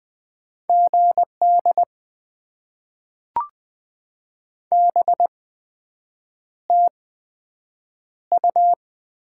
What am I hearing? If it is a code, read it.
Morse code, heard as a keyed tone, about 20 wpm, spaced as for 7 wpm, 700 Hz. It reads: GD BTU
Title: Top 90 QSO Elements - 8x Character Spacing - 20wpm